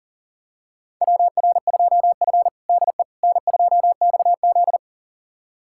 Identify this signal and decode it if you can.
Morse code, heard as a keyed tone, about 40 wpm, 700 Hz. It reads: WR2F DE N2XZ
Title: USA Callsign DE Callsign - 4 Characters - 40wpm